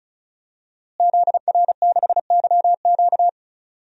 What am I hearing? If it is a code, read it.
Morse code, heard as a keyed tone, about 36 wpm, 700 Hz. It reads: ZR6YQ
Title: DXCC-100 Call Signs - 5 Characters - Repeated 3x - 35wpm